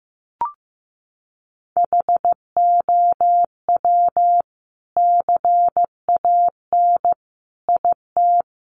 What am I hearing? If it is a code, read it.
Morse code, heard as a keyed tone, about 15 wpm, 700 Hz. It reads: HOW CAN IT
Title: Sentences from Top 1300 Words 15wpm